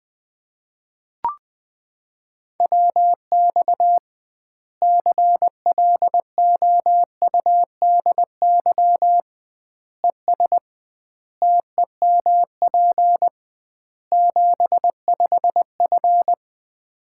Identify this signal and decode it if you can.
Morse code, heard as a keyed tone, about 20 wpm, 700 Hz. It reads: WX CLOUDY ES TEMP 75F
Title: Common QSO Phrases - 2x Word-Spacing - 20wpm